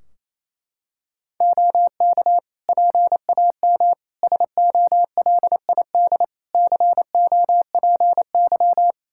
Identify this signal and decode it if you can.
Morse code, heard as a keyed tone, about 28 wpm, 700 Hz. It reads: OK PAM SOLID COPY
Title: Practice QSO - Exchange 3 - More Info - 28wpm